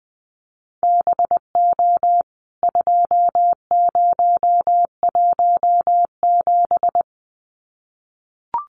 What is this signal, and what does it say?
Morse code, heard as a keyed tone, about 20 wpm, 700 Hz. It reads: BO 2017
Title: CWT Exchange 20wpm